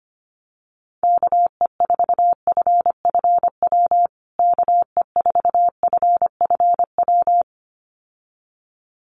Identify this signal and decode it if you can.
Morse code, heard as a keyed tone, about 25 wpm, 700 Hz. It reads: KE4FFW KE4FFW
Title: USA Call Signs Repeated 2x 25wpm